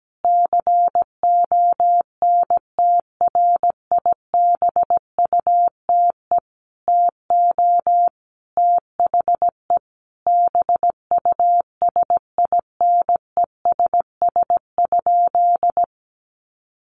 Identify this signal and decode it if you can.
Morse code, heard as a keyed tone, about 17 wpm, 700 Hz. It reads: CONTRIBUTE TO THE BUSINESS?
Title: Sentences from Top 1400 Words 17wpm